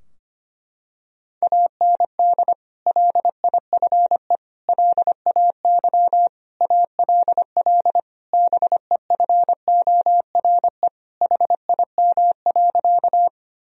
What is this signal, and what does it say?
Morse code, heard as a keyed tone, about 25 wpm, 700 Hz. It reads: AND LIFE LAY ALL BEFORE HIM.
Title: Sentences from Top 1000 Words - Review Part 2 - 25wpm